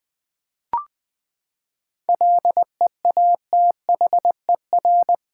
Morse code, heard as a keyed tone, about 20 wpm, 700 Hz. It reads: LEATHER
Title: Seven Letter Words - 20wpm